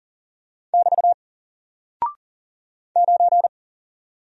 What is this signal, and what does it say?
Morse code, heard as a keyed tone, about 40 wpm, 700 Hz. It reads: <BT> 9
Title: Single Character (Letter:Number:Punctuation:Prosign) 40wpm